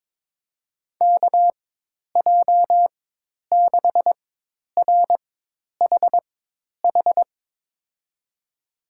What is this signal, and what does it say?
Morse code, heard as a keyed tone, about 22 wpm, spaced as for 10 wpm, 700 Hz. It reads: KJ6RHH
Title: USA Call Signs - 6 Characters - 4x Character Spacing - 22wpm